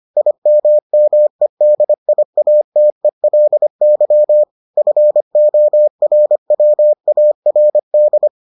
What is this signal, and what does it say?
Morse code, heard as a keyed tone, about 25 wpm, 600 Hz. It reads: IMMEDIATELY FORWARD